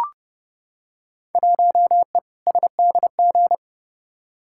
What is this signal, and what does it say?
Morse code, heard as a keyed tone, about 30 wpm, 700 Hz. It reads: 1E SDG
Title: Field Day Exchanges - 30wpm